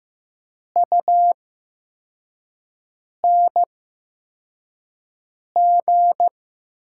Morse code, heard as a keyed tone, about 15 wpm, spaced as for 4 wpm, 700 Hz. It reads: UNG
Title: 8x Character Spacing - Top 500 Words - 15wpm